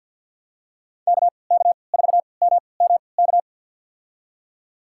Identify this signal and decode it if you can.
Morse code, heard as a keyed tone, about 52 wpm, spaced as for 29 wpm, 700 Hz. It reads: KK4MMK
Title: USA Call Signs - 6 Characters - 3x Character Spacing - 50wpm